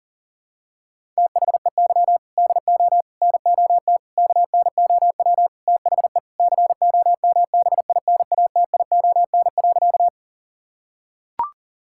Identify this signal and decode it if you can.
Morse code, heard as a keyed tone, about 40 wpm, 690 Hz. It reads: THEY DO NOT KNOW THE COMBINATION.
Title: Sentences from Top 1000 Words - Review Part 6 - 40wpm